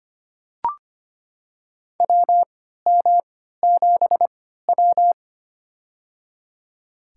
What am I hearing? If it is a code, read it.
Morse code, heard as a keyed tone, about 25 wpm, spaced as for 14 wpm, 700 Hz. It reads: WM7W
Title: USA Call Signs - 3x Character Spacing - 25wpm